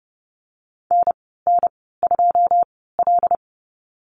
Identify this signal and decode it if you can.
Morse code, heard as a keyed tone, about 30 wpm, spaced as for 17 wpm, 700 Hz. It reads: NN2L
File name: Lesson 14g - Number 2 - Call Signs 30wpm